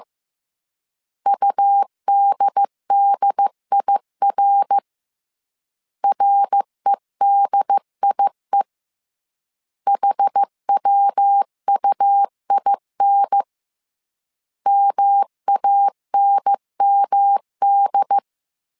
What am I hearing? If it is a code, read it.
Morse code, heard as a keyed tone, about 15 wpm, 800 Hz. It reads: UDDIR REDIE HWUIN MANMD